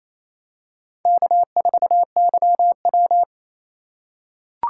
Morse code, heard as a keyed tone, about 28 wpm, 700 Hz. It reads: K4YW
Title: USA Call Signs - 4 Characters - 3x Character Spacing - 28wpm